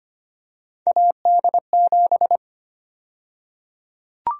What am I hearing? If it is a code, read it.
Morse code, heard as a keyed tone, about 25 wpm, 700 Hz. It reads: AD7
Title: USA Call Sign Prefixes - 25wpm